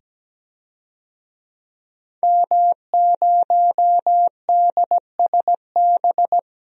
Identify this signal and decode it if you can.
Morse code, heard as a keyed tone, about 17 wpm, 700 Hz. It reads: M0DSB
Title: DXCC-100 Call Signs - 5 Characters - 17wpm